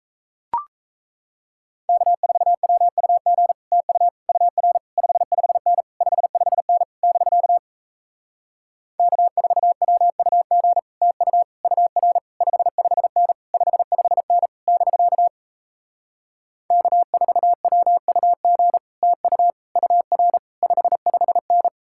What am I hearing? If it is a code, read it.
Morse code, heard as a keyed tone, about 38 wpm, 700 Hz. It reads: K4WUG TU UR 55N 55N <BK> K4WUG TU UR 55N 55N <BK> K4WUG TU UR 55N